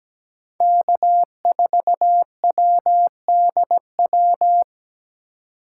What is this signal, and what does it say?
Morse code, heard as a keyed tone, about 17 wpm, 700 Hz. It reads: K4WDW
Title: USA Call Signs - 5 Characters - 2x Character Spacing - 17wpm